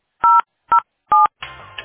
DTMF, heard as telephone touch tones, tones of about 140 ms. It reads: 007